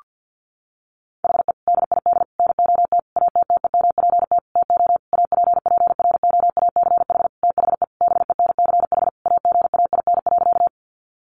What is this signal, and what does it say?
Morse code, heard as a keyed tone, about 50 wpm, 700 Hz. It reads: HE DID NOT ATTEMPT TO APPROACH THE BENCH AGAIN.
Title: Sentences from Top 1400 Words 50wpm